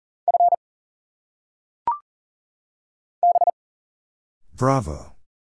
Morse code, heard as a keyed tone, about 40 wpm, spaced as for 7 wpm, 700 Hz. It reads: F BT
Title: Single Letters - Phonetic Pronunciation 40wpm